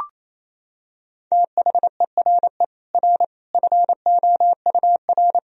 Morse code, heard as a keyed tone, about 28 wpm, 700 Hz. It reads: THERE R FOUR
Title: Sentences from Top 400 Words and Abbreviations 28wpm